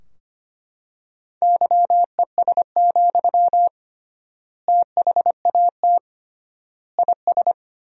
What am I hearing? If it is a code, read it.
Morse code, heard as a keyed tone, about 25 wpm, 700 Hz. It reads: YES, THAT IS
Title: 3x Word-Spacing - Sentences from Top 500 Words - 25wpm